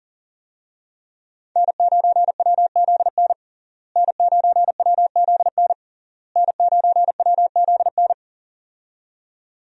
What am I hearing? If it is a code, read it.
Morse code, heard as a keyed tone, about 40 wpm, 700 Hz. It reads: N9WZN N9WZN N9WZN
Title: USA Call Signs - 5 Characters - Repeated 3x - 40wpm